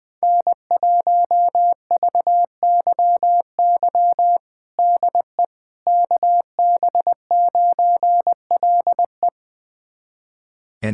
Morse code, heard as a keyed tone, about 20 wpm, 700 Hz. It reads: N1VYY DE KB9LE
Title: USA Callsign DE Callsign - 5 Characters - 20wpm